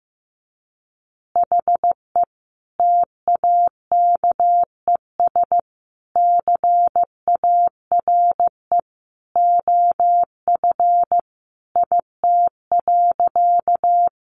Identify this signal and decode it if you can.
Morse code, heard as a keyed tone, about 15 wpm, 700 Hz. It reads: HE TAKES CARE OF IT.
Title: Additional Easy Phrases 15wpm